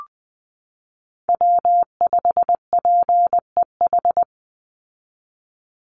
Morse code, heard as a keyed tone, about 20 wpm, 700 Hz. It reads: W5PEH
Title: Call Signs Repeated 2x - 20wpm